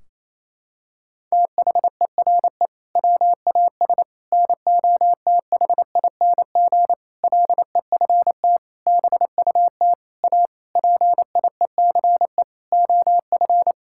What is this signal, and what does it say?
Morse code, heard as a keyed tone, about 28 wpm, 700 Hz. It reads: THERE WAS NOTHING LEFT BUT A PIECE OF
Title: Sentences from Top 1600 Words 28wpm